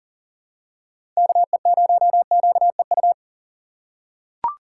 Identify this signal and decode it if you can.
Morse code, heard as a keyed tone, about 39 wpm, 700 Hz. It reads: KE0QEU
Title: USA Call Signs - 6 Characters - 6x Character Spacing - 40wpm